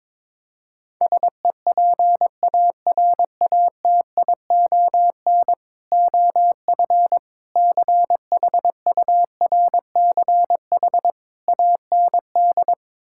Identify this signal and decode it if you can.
Morse code, heard as a keyed tone, about 22 wpm, 700 Hz. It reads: SEPARATION OF CHURCH AND
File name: practice / 22wpm / Build A Phrase - 5-Word Phrases - 22wpm